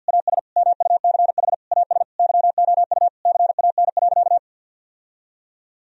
Morse code, heard as a keyed tone, about 51 wpm, 700 Hz. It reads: AS MUCH AS YOU CAN.
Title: Sentences from Top 1000 Words - Review Part 1 - 50wpm